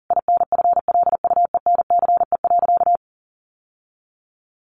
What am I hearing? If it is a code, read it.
Morse code, heard as a keyed tone, about 40 wpm, 700 Hz. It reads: INFLUENCE.
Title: Sentences from Top 1100 Words 40wpm